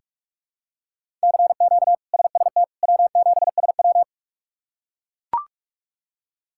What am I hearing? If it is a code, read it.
Morse code, heard as a keyed tone, about 41 wpm, 700 Hz. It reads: CQ SST W7SW
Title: K1USN SST QSO Practice - 45wpm